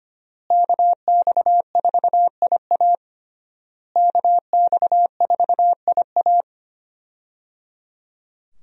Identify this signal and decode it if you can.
Morse code, heard as a keyed tone, about 25 wpm, 700 Hz. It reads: KX4IA KX4IA
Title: USA Call Signs - 5 Characters - Repeated 3x - 25wpm